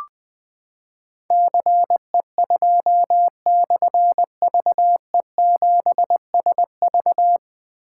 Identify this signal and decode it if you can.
Morse code, heard as a keyed tone, about 21 wpm, 700 Hz. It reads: CE2/VE7SV
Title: Difficult Call Signs 20wpm